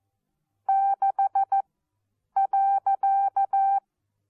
Morse code, heard as a keyed tone, about 14 wpm, spaced as for 8 wpm, 780 Hz. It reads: 6.
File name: Onlinelehrgang / morsen / DJ4UF Morsen14